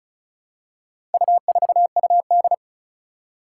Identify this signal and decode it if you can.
Morse code, heard as a keyed tone, about 36 wpm, 700 Hz. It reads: U4UD
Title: Four Challenging Letter-Number Combinations 35wpm